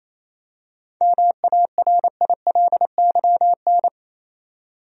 Morse code, heard as a keyed tone, about 28 wpm, 700 Hz. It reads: MARILYN
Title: USA Common Names - 28wpm